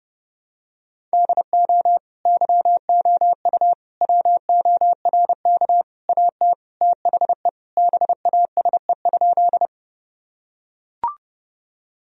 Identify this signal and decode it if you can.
Morse code, heard as a keyed tone, about 30 wpm, 700 Hz. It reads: DO YOU WORK AT THE BASE?
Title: Sentences from Top 1000 Words - Review Part 1 - 30wpm